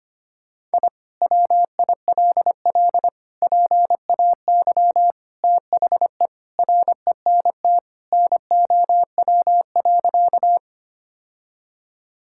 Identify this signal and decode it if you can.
Morse code, heard as a keyed tone, about 25 wpm, 700 Hz. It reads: I WILL PAY THE RENT NOW.
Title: Sentences from Top 1000 Words - Review Part 5 - 25wpm